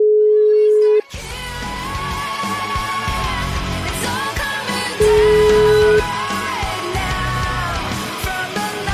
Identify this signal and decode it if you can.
Ringback tone (Europe, 425 Hz), with noise.